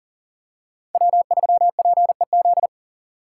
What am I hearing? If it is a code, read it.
Morse code, heard as a keyed tone, about 40 wpm, 700 Hz. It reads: W3PEZ
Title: USA Call Signs - 5 Characters - 4x Character Spacing - 40wpm